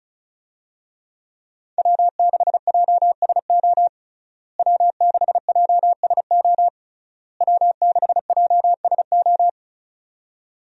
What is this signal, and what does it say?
Morse code, heard as a keyed tone, about 36 wpm, 700 Hz. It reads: W6JSO W6JSO W6JSO
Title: USA Call Signs - 5 Characters - Repeated 3x - 35wpm